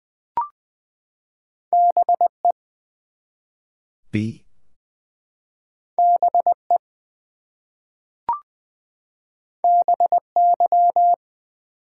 Morse code, heard as a keyed tone, about 20 wpm, 700 Hz. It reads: BE BE BY
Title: Two Letter Words - 20wpm